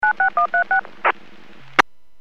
DTMF, heard as telephone touch tones, tones of about 110 ms. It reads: 96136
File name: WPS / 00540 13-34 WPS 451.25 WPS